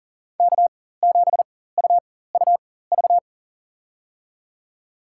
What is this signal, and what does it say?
Morse code, heard as a keyed tone, about 40 wpm, spaced as for 19 wpm, 700 Hz. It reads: K7UUV
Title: USA Call Signs - 5 Characters - 4x Character Spacing - 40wpm